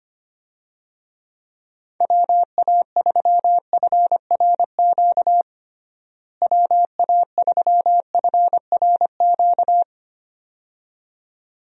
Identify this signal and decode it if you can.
Morse code, heard as a keyed tone, about 25 wpm, 700 Hz. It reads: WA3FRQ WA3FRQ
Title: USA Call Signs - 6 Characters - Repeated 2x - 25wpm